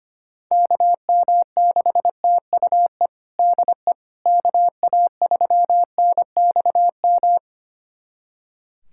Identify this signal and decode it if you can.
Morse code, heard as a keyed tone, about 25 wpm, 700 Hz. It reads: KM6TUE DE KA3NXM
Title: USA Callsign DE Callsign 25wpm